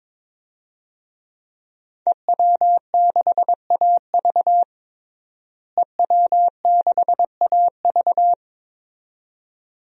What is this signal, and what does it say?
Morse code, heard as a keyed tone, about 22 wpm, 700 Hz. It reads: EW6AV EW6AV
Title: DXCC-100 Call Signs - 5 Characters - Repeated 2x - 22wpm